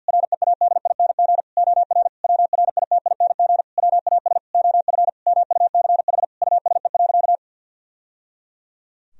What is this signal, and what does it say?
Morse code, heard as a keyed tone, about 50 wpm, 700 Hz. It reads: READING OR WRITING WAS OF MUCH USE.
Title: Sentences from Top 1000 Words - Review Part 3 - 50wpm